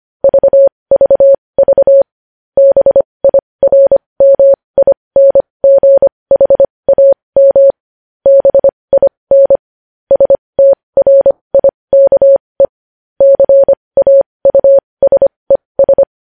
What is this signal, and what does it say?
Morse code, heard as a keyed tone, about 25 wpm, spaced as for 20 wpm, 550 Hz. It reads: VVV BIRMINGHAM BIN STRIKE CAUSES
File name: podcast / News headlines-1x-25x20